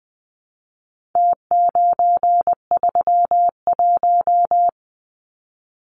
Morse code, heard as a keyed tone, about 20 wpm, 700 Hz. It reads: T931
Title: Numbers - Contest Style - 4-Digit 20wpm